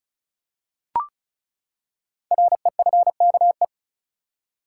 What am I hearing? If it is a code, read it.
Morse code, heard as a keyed tone, about 36 wpm, 700 Hz. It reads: REFKE
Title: Code Groups - 5 Letter:Numbers - 35wpm